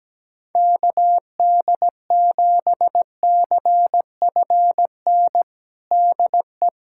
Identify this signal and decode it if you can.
Morse code, heard as a keyed tone, about 17 wpm, 700 Hz. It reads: KD7CFN DE
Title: USA Callsign DE Callsign 17wpm